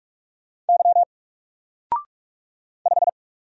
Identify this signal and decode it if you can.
Morse code, heard as a keyed tone, about 46 wpm, 700 Hz. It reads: Y 5
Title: Single Letter-Number 45wpm